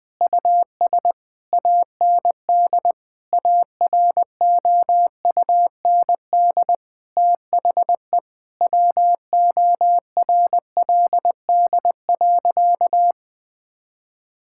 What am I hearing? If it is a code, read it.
Morse code, heard as a keyed tone, about 21 wpm, 700 Hz. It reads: US AND AROUND THE WORLD.
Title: Sentences with 7-Word Phrases - Spoken Beginning - 20wpm